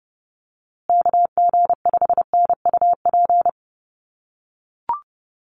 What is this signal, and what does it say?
Morse code, heard as a keyed tone, about 30 wpm, 700 Hz. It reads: KG5NUP
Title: USA Call Signs Repeated 2x 30wpm